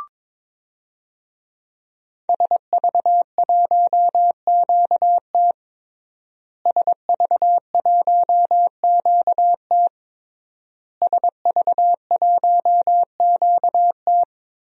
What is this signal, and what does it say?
Morse code, heard as a keyed tone, about 22 wpm, 700 Hz. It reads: SV1QT SV1QT SV1QT
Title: DXCC-100 Call Signs - 5 Characters - Repeated 3x - 22wpm